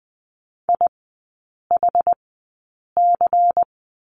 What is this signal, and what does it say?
Morse code, heard as a keyed tone, about 20 wpm, spaced as for 8 wpm, 690 Hz. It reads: IHC
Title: 093 Lesson 16e - Letter c - Three Characters 20wpm